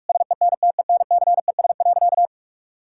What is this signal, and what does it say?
Morse code, heard as a keyed tone, about 46 wpm, 700 Hz. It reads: SENTENCES.
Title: Pangram Challenge v2 - 45wpm